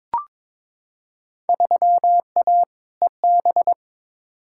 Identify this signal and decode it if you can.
Morse code, heard as a keyed tone, about 22 wpm, 700 Hz. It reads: 3A EB
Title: Field Day Exchanges Encore - 22wpm